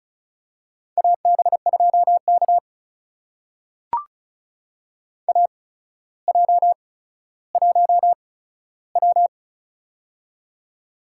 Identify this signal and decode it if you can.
Morse code, heard as a keyed tone, about 35 wpm, spaced as for 15 wpm, 700 Hz. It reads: AB2K AJ1W
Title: USA Call Signs - 4 Characters - 8x Character Spacing - 35wpm